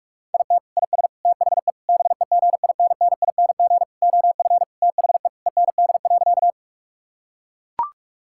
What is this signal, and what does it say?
Morse code, heard as a keyed tone, about 41 wpm, 700 Hz. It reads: IT IS THE BEGINNING OF THE END.
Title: Sentences from Top 1000 Words - Review Part 5 - 45wpm